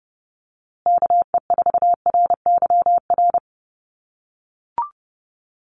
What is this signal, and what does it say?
Morse code, heard as a keyed tone, about 30 wpm, 700 Hz. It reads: KE4RYR
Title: USA Call Signs Repeated 3x 30wpm